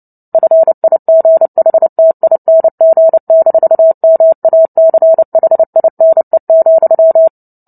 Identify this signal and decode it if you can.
Morse code, heard as a keyed tone, about 30 wpm, 650 Hz. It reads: FIGHTING-MACHINE,
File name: Book Two / 03-Chapter 3 - 29 WPM